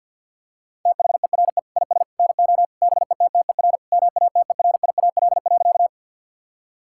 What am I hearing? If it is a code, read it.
Morse code, heard as a keyed tone, about 50 wpm, 700 Hz. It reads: THERE IS NO BETTER MATERIAL.
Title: Sentences from Top 1000 Words - Review Part 5 - 50wpm